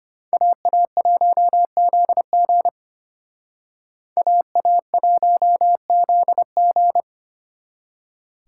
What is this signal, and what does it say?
Morse code, heard as a keyed tone, about 32 wpm, 700 Hz. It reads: AA1ZG AA1ZG